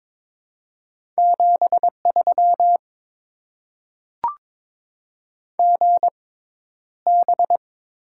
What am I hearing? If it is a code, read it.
Morse code, heard as a keyed tone, about 22 wpm, spaced as for 11 wpm, 700 Hz. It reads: 73 GB
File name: Top 90 QSO Elements - 6x Character Spacing - 22wpm